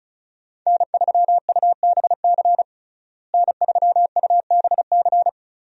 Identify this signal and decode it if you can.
Morse code, heard as a keyed tone, about 34 wpm, 700 Hz. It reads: N3UBC N3UBC
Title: USA Call Signs - 5 Characters - Repeated 3x - 35wpm